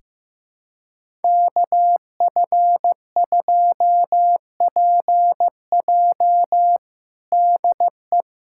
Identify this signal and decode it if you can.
Morse code, heard as a keyed tone, about 16 wpm, 700 Hz. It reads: KF2PJ DE
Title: USA Callsign DE Callsign - 5 Characters - 15wpm